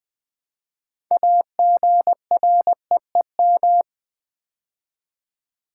Morse code, heard as a keyed tone, about 20 wpm, 700 Hz. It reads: AGREEM
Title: Build A Word - 3 Letters At A Time - 9-Letter Words - 20wpm